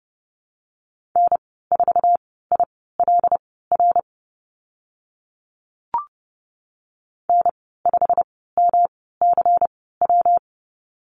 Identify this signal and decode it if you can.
Morse code, heard as a keyed tone, about 30 wpm, spaced as for 17 wpm, 700 Hz. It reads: N4ILR N5MCW
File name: Lesson 19g - Letter w - Call Signs 30wpm